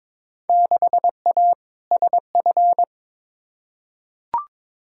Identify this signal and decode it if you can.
Morse code, heard as a keyed tone, about 21 wpm, 700 Hz. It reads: 6A SF
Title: Field Day Exchanges - 22wpm